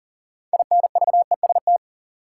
Morse code, heard as a keyed tone, about 40 wpm, 700 Hz. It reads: INVEST
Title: Sentences from Top 1100 Words - New Word Drill Alt - 40wpm